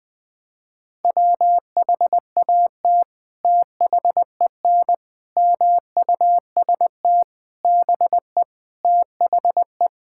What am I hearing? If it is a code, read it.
Morse code, heard as a keyed tone, about 20 wpm, 700 Hz. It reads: WHAT THEN MUST BE THE
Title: Sentences from Top 1000 Words - Review Part 5 - 20wpm